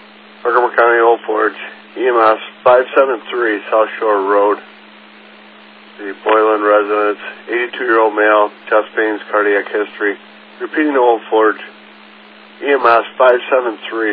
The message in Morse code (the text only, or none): none